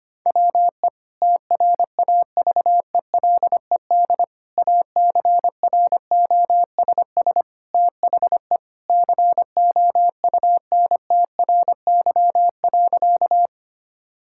Morse code, heard as a keyed tone, about 25 wpm, 700 Hz. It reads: WE TRAVELED ACROSS THE COUNTRY.
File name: practice / 25wpm / Sentences with 2-Word Phrases - 25wpm